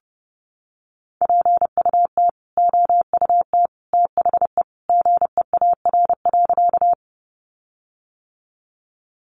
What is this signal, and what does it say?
Morse code, heard as a keyed tone, about 28 wpm, 700 Hz. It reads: PUT OUT THE GEAR.
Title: Sentences from Top 1400 Words 30wpm